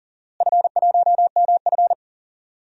Morse code, heard as a keyed tone, about 40 wpm, 700 Hz. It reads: F1MF
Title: DXCC-100 Call Signs - 4 Characters - Repeated 2x - 40wpm